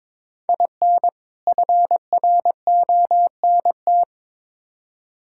Morse code, heard as a keyed tone, about 22 wpm, 700 Hz. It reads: IN FRONT
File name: Build A Phrase - 2-Word Phrases - 22wpm